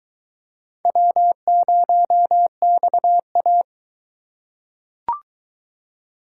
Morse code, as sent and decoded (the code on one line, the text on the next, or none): .-- ----- -..- .-
W0XA